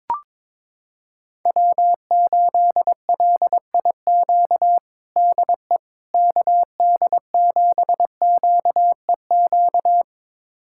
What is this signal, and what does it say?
Morse code, heard as a keyed tone, about 22 wpm, 700 Hz. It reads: W8LIQ DE KD7QEQ
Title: USA Callsign DE Callsign 22wpm